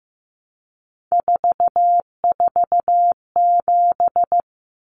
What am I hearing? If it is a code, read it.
Morse code, heard as a keyed tone, about 15 wpm, 700 Hz. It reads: 447
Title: Common Cut Numbers - 3-Digit 15wpm